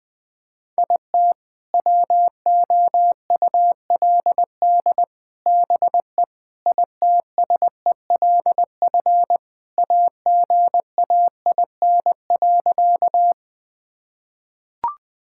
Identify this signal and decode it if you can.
Morse code, heard as a keyed tone, about 20 wpm, 700 Hz. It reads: IT WOULD BE ITSELF AGAIN.
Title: Sentences from Top 1000 Words - Review Part 5 - 20wpm